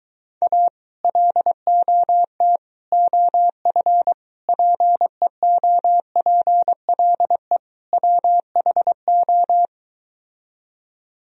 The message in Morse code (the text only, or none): A LOT OF PEOPLE WHO